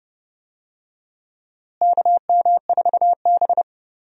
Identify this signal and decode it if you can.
Morse code, heard as a keyed tone, about 30 wpm, 700 Hz. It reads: KM4B
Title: USA Call Signs - 4 Characters - Repeated 3x - 30wpm